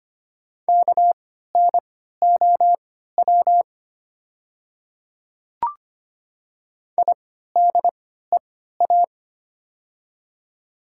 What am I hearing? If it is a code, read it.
Morse code, heard as a keyed tone, about 25 wpm, spaced as for 14 wpm, 690 Hz. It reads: KNOW IDEA
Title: 3x Character Spacing - Top 300 Words - 25wpm